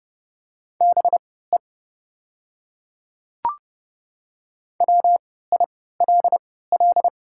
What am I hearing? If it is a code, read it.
Morse code, heard as a keyed tone, about 30 wpm, spaced as for 17 wpm, 700 Hz. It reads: BE WILL
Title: 3x Character Spacing - Top 100 Words - 30wpm